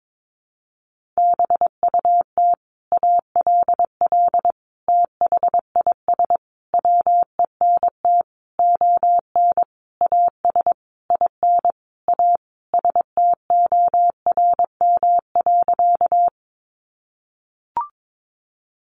Morse code, heard as a keyed tone, about 22 wpm, 700 Hz. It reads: BUT ALL THIS WENT ON AS IN A STORM.
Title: Sentences from Top 1500 Words 22wpm